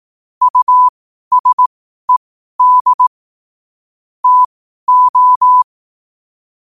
Morse code, heard as a keyed tone, about 18 wpm, spaced as for 12 wpm, 1000 Hz. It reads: USED TO
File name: qotdfiles / 12wpm